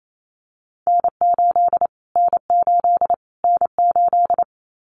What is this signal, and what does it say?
Morse code, heard as a keyed tone, about 28 wpm, 700 Hz. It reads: N8 N8 N8
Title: Callsign Prefixes for Top 25 DXCC - Repeated 3x - 28wpm